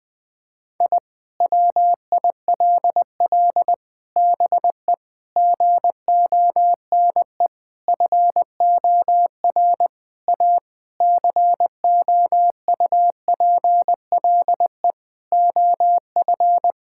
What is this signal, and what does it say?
Morse code, heard as a keyed tone, about 20 wpm, 700 Hz. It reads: I WILL BE GONE FOR A COUPLE OF
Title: Sentences with 4-Word Phrases - Spoken Beginning - 20wpm